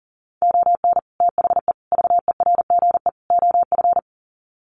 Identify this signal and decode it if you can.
Morse code, heard as a keyed tone, about 40 wpm, 700 Hz. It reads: ON THE VERGE OF
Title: Sets of 4 Words 40wpm